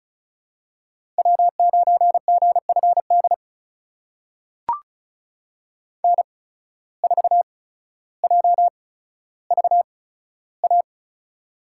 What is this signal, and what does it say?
Morse code, heard as a keyed tone, about 36 wpm, spaced as for 15 wpm, 700 Hz. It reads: W9GFD N4JVA
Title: USA Call Signs - 5 Characters - 8x Character Spacing - 35wpm